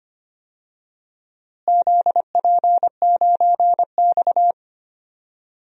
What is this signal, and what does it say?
Morse code, heard as a keyed tone, about 25 wpm, 700 Hz. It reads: ZP9X
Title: DXCC-100 Call Signs - 4 Characters - Repeated 2x - 25wpm